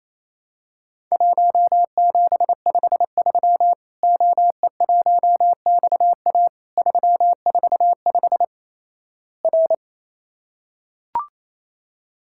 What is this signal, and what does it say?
Morse code, heard as a keyed tone, about 28 wpm, 700 Hz. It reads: 1753 OE1XA 345 R
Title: Worked All Europe DX Contest - QTC Exchange - 28wpm